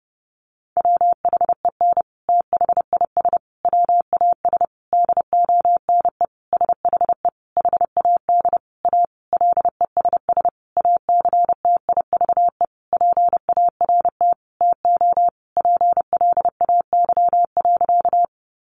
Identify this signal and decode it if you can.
Morse code, heard as a keyed tone, about 30 wpm, 700 Hz. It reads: WHEN THIS WAS DONE SHE HAD A LESS ACTIVE PART TO PLAY.
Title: Sentences from Top 900 Words 30wpm